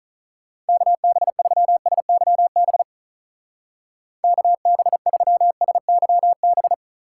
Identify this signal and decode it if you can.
Morse code, heard as a keyed tone, about 45 wpm, 700 Hz. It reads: KB3SYB KB3SYB